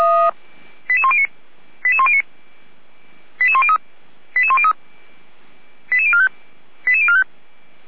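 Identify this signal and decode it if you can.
Five-tone selective call (ZVEI), heard as quick runs of five tones, tones of about 70 ms. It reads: 88109 88109 88103 88103 88045 88045
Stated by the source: ZVEI five-tone call 88190 received by SDR, 455 channel.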